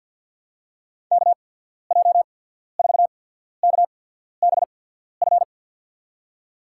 Morse code, heard as a keyed tone, about 48 wpm, spaced as for 14 wpm, 690 Hz. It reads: KJ4KBF
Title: USA Call Signs - 6 Characters - 8x Character Spacing - 50wpm